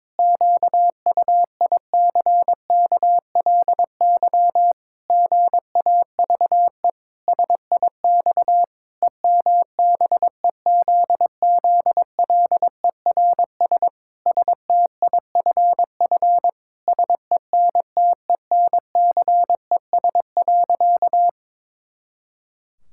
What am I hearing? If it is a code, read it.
Morse code, heard as a keyed tone, about 21 wpm, 700 Hz. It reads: QUICKLY GAVE SIX EMBEZZLERS STIFF SENTENCES.
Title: Pangram Challenge v2 - 22wpm